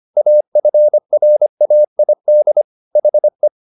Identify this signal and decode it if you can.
Morse code, heard as a keyed tone, about 25 wpm, 600 Hz. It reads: AFRAID HE
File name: the old man and the sea part 1 25wpm